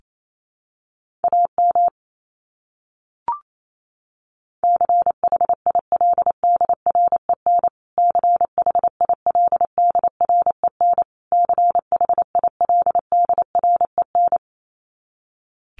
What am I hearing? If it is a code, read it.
Morse code, heard as a keyed tone, about 28 wpm, 700 Hz. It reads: AM CHILDREN CHILDREN CHILDREN
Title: Top 500 Words Repeated 28wpm